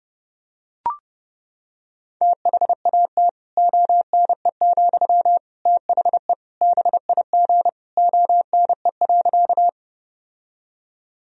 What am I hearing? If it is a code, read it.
Morse code, heard as a keyed tone, about 32 wpm, 700 Hz. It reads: THAT ONE, THE BIG ONE.
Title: Sentences from Top 1000 Words - Review Part 1 - 30wpm